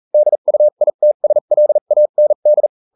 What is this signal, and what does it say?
Morse code, heard as a keyed tone, about 40 wpm, spaced as for 35 wpm, 600 Hz. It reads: DUITSLAND